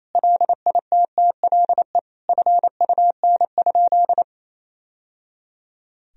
Morse code, heard as a keyed tone, about 28 wpm, 700 Hz. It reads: LITTLE FUN?
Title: Sentences from Top 1000 Words - Review Part 1 - 28wpm